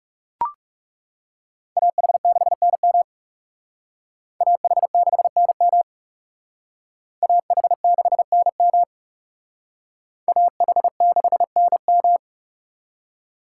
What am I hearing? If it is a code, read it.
Morse code, heard as a keyed tone, about 38 wpm, 700 Hz. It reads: AH6NM AH6NM AH6NM AH6NM
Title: Speed Racing - USA 2x2 Call Signs 30wpm